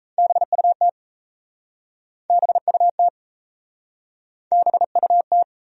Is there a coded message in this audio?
Yes: Morse code from 0.2 to 5.4 s, heard as a keyed tone, about 38 wpm, 700 Hz. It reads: BUT BUT BUT